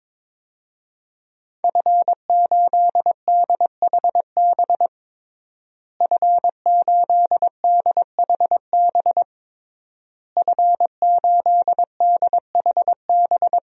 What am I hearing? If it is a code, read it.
Morse code, heard as a keyed tone, about 22 wpm, 700 Hz. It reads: F8DHB F8DHB F8DHB
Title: DXCC-100 Call Signs - 5 Characters - Repeated 3x - 22wpm